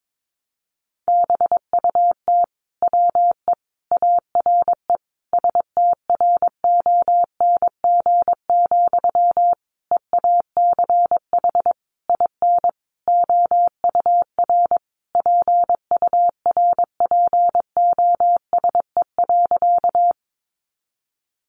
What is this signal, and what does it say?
Morse code, heard as a keyed tone, about 22 wpm, 700 Hz. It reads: BUT WE ARE STRONG, EACH IN OUR PURPOSE.
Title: Sentences from Top 800 Words 22wpm